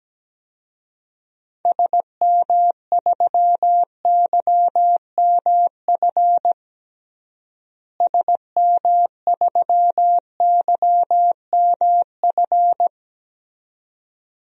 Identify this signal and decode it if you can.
Morse code, heard as a keyed tone, about 17 wpm, 700 Hz. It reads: SM3YMF SM3YMF
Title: DXCC-100 Call Signs - 6 Characters - Repeated 2x - 17wpm